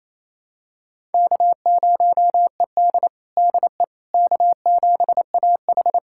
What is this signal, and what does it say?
Morse code, heard as a keyed tone, about 28 wpm, 700 Hz. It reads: K0ED DE K7AH
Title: USA Callsign DE Callsign - 4 Characters - 28wpm